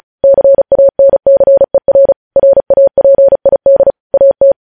Morse code, heard as a keyed tone, about 36 wpm, 550 Hz. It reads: CANCER RAPID AT
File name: podcast / News headlines-1x-35x35